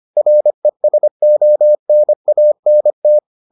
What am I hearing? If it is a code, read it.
Morse code, heard as a keyed tone, about 25 wpm, 600 Hz. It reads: RESONANT